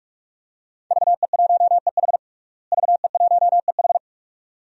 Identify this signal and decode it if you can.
Morse code, heard as a keyed tone, about 48 wpm, 700 Hz. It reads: VE1EH VE1EH
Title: DXCC-100 Call Signs - 5 Characters - Repeated 2x - 45wpm